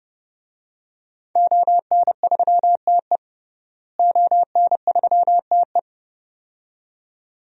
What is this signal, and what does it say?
Morse code, heard as a keyed tone, about 30 wpm, 700 Hz. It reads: ON3TE ON3TE
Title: DXCC-100 Call Signs - 5 Characters - Repeated 2x - 30wpm